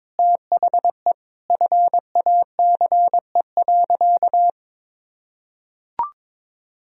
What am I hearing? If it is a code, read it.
Morse code, heard as a keyed tone, about 22 wpm, 700 Hz. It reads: THE FACE.
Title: Sentences from Top 1000 Words - Review Part 2 - 22wpm